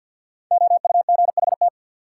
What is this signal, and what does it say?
Morse code, heard as a keyed tone, about 50 wpm, 700 Hz. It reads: OUGHT
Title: Sets of 5 Words Encore - New Word Drill - 50wpm